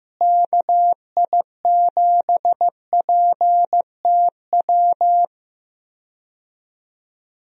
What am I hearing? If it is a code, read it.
Morse code, heard as a keyed tone, about 15 wpm, 700 Hz. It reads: KI7PTW
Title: USA Call Signs - 6 Characters - Repeated 2x - 15wpm